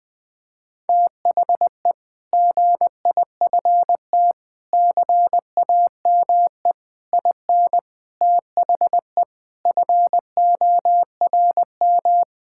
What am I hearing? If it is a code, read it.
Morse code, heard as a keyed tone, about 20 wpm, 700 Hz. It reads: THE GIFT CAME IN THE FORM
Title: Sentences with 5-Word Phrases - 20wpm